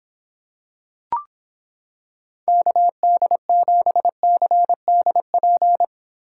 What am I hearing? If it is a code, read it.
Morse code, heard as a keyed tone, about 26 wpm, 700 Hz. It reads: KD7CDP